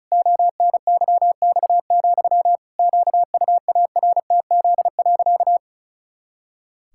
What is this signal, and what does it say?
Morse code, heard as a keyed tone, about 34 wpm, 700 Hz. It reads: ONYX, QUARTZ.
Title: Pangram Challenge v2 - 35wpm